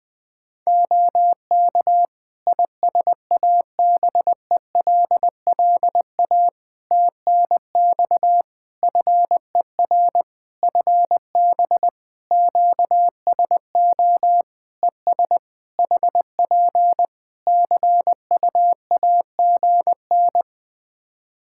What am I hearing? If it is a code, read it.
Morse code, heard as a keyed tone, about 20 wpm, 700 Hz. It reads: OK ISABELLA TNX FER FB QSO ES HP CUAGN
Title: Practice QSO - Exchange 4 - Ending - 20wpm